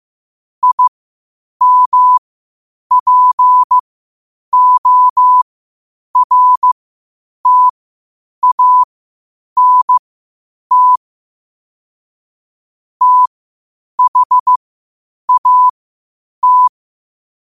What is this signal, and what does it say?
Morse code, heard as a keyed tone, about 15 wpm, spaced as for 8 wpm, 1000 Hz. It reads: IMPORTANT THAT
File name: qotdfiles / 8wpm